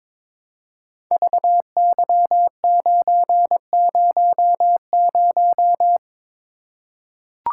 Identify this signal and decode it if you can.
Morse code, heard as a keyed tone, about 21 wpm, 700 Hz. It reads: VY900
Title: Code Groups - 5 Letter:Numbers - 22wpm